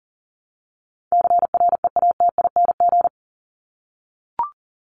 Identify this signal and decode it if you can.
Morse code, heard as a keyed tone, about 40 wpm, 700 Hz. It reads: CREATING
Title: ING Suffix - Top 100 Words with Base Words 40wpm